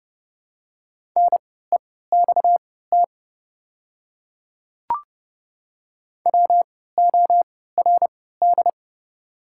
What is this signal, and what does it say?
Morse code, heard as a keyed tone, about 31 wpm, spaced as for 17 wpm, 700 Hz. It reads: NEXT WORD
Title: 3x Character Spacing - Top 300 Words - 30wpm